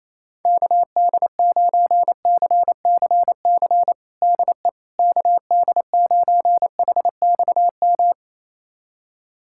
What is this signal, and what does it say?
Morse code, heard as a keyed tone, about 28 wpm, 700 Hz. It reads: KD9CCC DE KD9HXM
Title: USA Callsign DE Callsign - 6 Characters - 28wpm